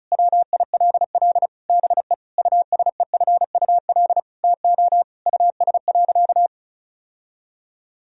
Morse code, heard as a keyed tone, about 34 wpm, 700 Hz. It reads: WILL BE USEFUL TO US.
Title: Sentences from Top 1000 Words - Review Part 4 - 35wpm